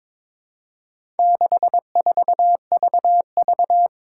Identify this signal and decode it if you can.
Morse code, heard as a keyed tone, about 22 wpm, 700 Hz. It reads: 64VV
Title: Four Challenging Letter-Number Combinations 22wpm